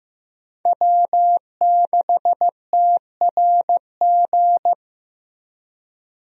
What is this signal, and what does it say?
Morse code, heard as a keyed tone, about 14 wpm, 700 Hz. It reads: W6TRG
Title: USA Call Signs - 5 Characters - Repeated 2x - 15wpm